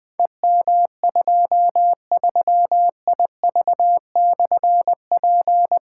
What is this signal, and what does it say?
Morse code, heard as a keyed tone, about 20 wpm, 700 Hz. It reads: EM23IV/P
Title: Difficult Call Signs 20wpm